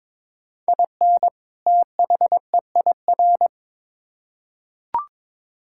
Morse code, heard as a keyed tone, about 21 wpm, 700 Hz. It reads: IN THEIR
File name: Build A Phrase - 2-Word Phrases - 22wpm